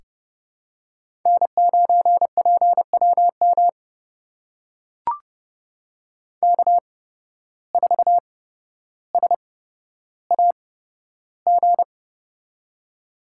Morse code, heard as a keyed tone, about 30 wpm, spaced as for 13 wpm, 700 Hz. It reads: N9PWM K4SAG
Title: USA Call Signs - 5 Characters - 8x Character Spacing - 30wpm